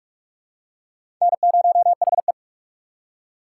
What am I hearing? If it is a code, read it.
Morse code, heard as a keyed tone, about 44 wpm, 700 Hz. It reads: N0HE
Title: USA Call Signs - 2x Character Spacing - 45wpm